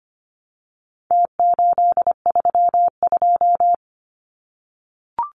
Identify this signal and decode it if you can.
Morse code, heard as a keyed tone, about 25 wpm, 700 Hz. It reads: T832
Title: Numbers - Contest Style - 4-Digit 25wpm